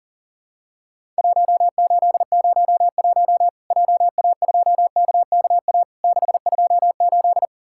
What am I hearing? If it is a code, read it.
Morse code, heard as a keyed tone, about 37 wpm, 700 Hz. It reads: 1801 JA2KKA 628
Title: Worked All Europe DX Contest - QTC Exchange - 40wpm